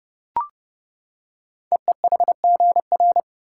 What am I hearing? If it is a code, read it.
Morse code, heard as a keyed tone, about 30 wpm, 690 Hz. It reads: EEHGR
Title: Code Groups - 5 Letters - 30wpm